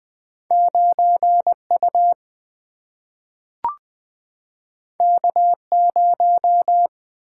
Morse code, heard as a keyed tone, about 20 wpm, 700 Hz. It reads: 9U K0
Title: Single-Letter Single-Number 20wpm